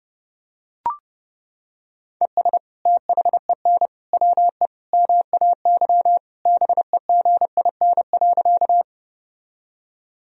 Morse code, heard as a keyed tone, about 30 wpm, 700 Hz. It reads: ES THEN WE MAY BEGIN.
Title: Sentences from Top 300 Words and Abbreviations 30wpm